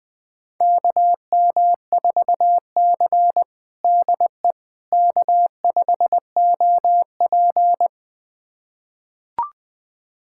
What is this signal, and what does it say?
Morse code, heard as a keyed tone, about 20 wpm, 700 Hz. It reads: KM4C DE K5OP
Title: USA Callsign DE Callsign - 4 Characters - 20wpm